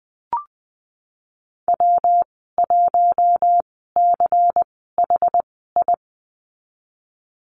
Morse code, heard as a keyed tone, about 20 wpm, spaced as for 15 wpm, 700 Hz. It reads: W1CHI
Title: 167 Lesson 27g - Number 7 - Call Signs 20wpm